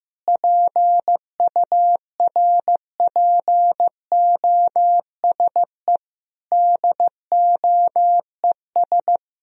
Morse code, heard as a keyed tone, about 15 wpm, 700 Hz. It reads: PURPOSE DOES
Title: Sentences from Top 1000 Words - Review Part 2 - 15wpm